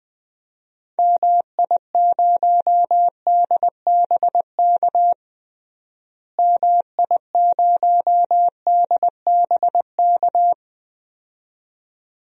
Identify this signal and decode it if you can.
Morse code, heard as a keyed tone, about 20 wpm, 700 Hz. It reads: MI0DBK MI0DBK
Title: DXCC-100 Call Signs - 6 Characters - Repeated 2x - 20wpm